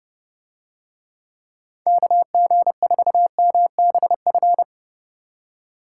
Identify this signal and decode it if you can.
Morse code, heard as a keyed tone, about 28 wpm, 700 Hz. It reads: KG4MBF